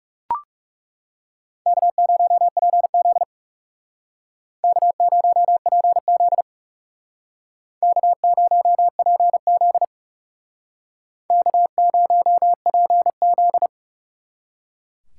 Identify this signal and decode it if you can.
Morse code, heard as a keyed tone, about 38 wpm, 700 Hz. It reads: K0PZ K0PZ K0PZ K0PZ